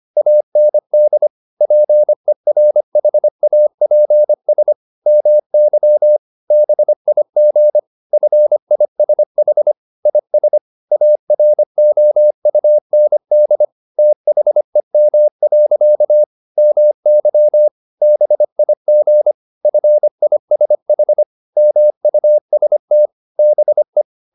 Morse code, heard as a keyed tone, about 25 wpm, 600 Hz. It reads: AND PERHAPS MY BIG FISH IS AROUND THEM. MY BIG FISH MUST BE
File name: the old man and the sea part 1 25wpm